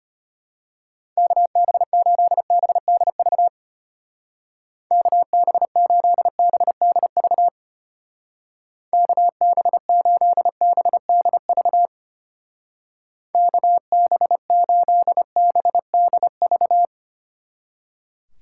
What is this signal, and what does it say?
Morse code, heard as a keyed tone, about 31 wpm, 700 Hz. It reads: KB8BDV KB8BDV KB8BDV KB8BDV
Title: Speed Racing - USA 2x3 Call Signs 25wpm